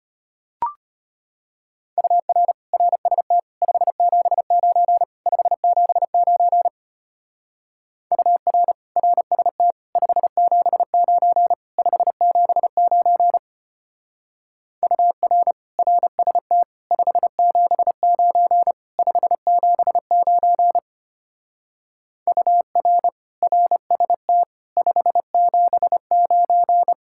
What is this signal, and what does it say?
Morse code, heard as a keyed tone, about 31 wpm, 700 Hz. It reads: UR RST 579 579 UR RST 579 579 UR RST 579 579 UR RST 579
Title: Speed-Racing Common QSO Phrases Encore - 25wpm